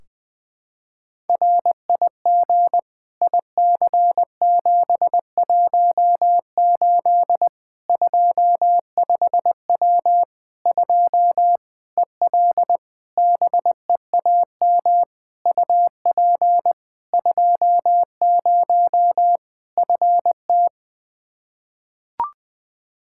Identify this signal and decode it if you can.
Morse code, heard as a keyed tone, about 20 wpm, 700 Hz. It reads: RIG IC718 25W 2 EL BEAM UP 20 FT
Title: Practice QSO - Exchange 2 - Info - 20wpm